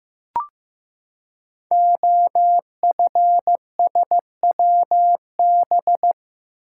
Morse code, heard as a keyed tone, about 15 wpm, 700 Hz. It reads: OFSWB
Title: Code Groups - 5 Letters - 15wpm